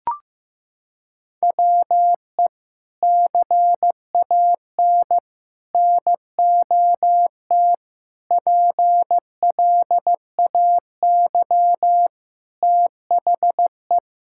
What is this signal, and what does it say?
Morse code, heard as a keyed tone, about 15 wpm, 700 Hz. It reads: WE CAN NOT PLAY THE
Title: Sentences from Top 1000 Words - Review Part 6 - 15wpm